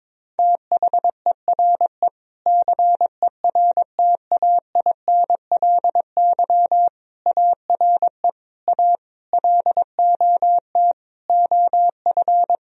Morse code, heard as a keyed tone, about 22 wpm, 700 Hz. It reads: THERE CERTAINLY ARE A LOT OF
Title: Sentences from Top 1000 Words - Review Part 3 - 22wpm